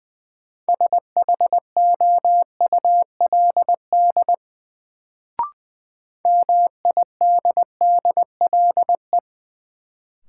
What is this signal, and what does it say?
Morse code, heard as a keyed tone, about 20 wpm, 700 Hz. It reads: SHOULD MIDDLE
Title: Sets of 2 Words - New Word Drill - 20wpm